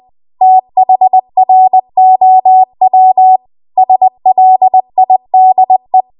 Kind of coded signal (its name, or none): Morse code